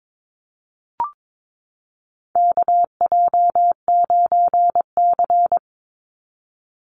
Morse code, heard as a keyed tone, about 22 wpm, 690 Hz. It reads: KJ9C
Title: Call Signs Repeated 3x - 22wpm